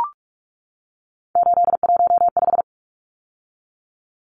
Morse code, heard as a keyed tone, about 44 wpm, 700 Hz. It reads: 815
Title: Numbers - 3-Digit 45wpm